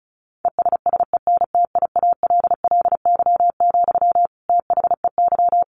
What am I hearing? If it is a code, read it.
Morse code, heard as a keyed tone, about 34 wpm, 700 Hz. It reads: ESSENTIALLY, THEY
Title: Sentences from Top 1300 Words 35wpm